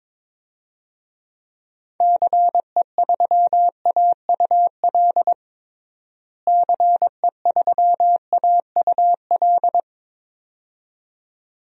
Morse code, heard as a keyed tone, about 22 wpm, 700 Hz. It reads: CE3AUL CE3AUL
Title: DXCC-100 Call Signs - 6 Characters - Repeated 2x - 22wpm